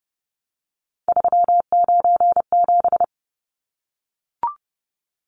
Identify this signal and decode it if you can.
Morse code, heard as a keyed tone, about 30 wpm, 700 Hz. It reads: 397
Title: Numbers - 3-Digit 30wpm